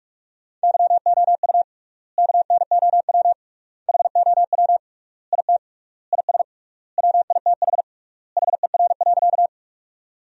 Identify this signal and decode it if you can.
Morse code, heard as a keyed tone, about 46 wpm, 700 Hz. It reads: YOU KNOW HOW IT IS WITH HER.
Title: 3x Word-Spacing - Sentences from Top 100 Words - 45wpm